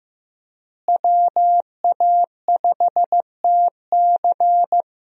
Morse code, heard as a keyed tone, about 16 wpm, 700 Hz. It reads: WA5TC
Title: USA Call Signs - 5 Characters - Repeated 3x - 15wpm